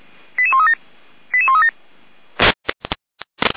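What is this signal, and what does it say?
Five-tone selective call (ZVEI), heard as quick runs of five tones, tones of about 70 ms. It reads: 88137 88137
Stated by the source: ZVEI five-tone selective call received with an SDR.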